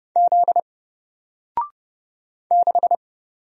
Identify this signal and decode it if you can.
Morse code, heard as a keyed tone, about 30 wpm, 700 Hz. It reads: Z 6
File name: Single-Letter-Number-30wpm